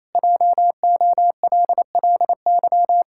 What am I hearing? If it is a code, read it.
Morse code, heard as a keyed tone, about 28 wpm, 700 Hz. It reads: JOLLY
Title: Pangram Challenge v2 - 28wpm